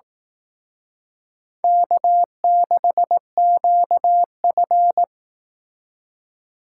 Morse code, heard as a keyed tone, about 18 wpm, 700 Hz. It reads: K6QF
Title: Speed Racing - USA 1x2 Call Signs 15wpm